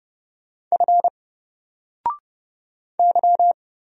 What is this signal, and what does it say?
Morse code, heard as a keyed tone, about 30 wpm, 700 Hz. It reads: F Y
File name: Single-Letter-Number-30wpm